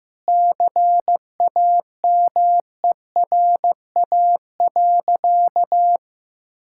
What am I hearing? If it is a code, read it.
Morse code, heard as a keyed tone, about 15 wpm, 700 Hz. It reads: CAMERA.
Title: Sentences from Top 1000 Words - Review Part 5 - 15wpm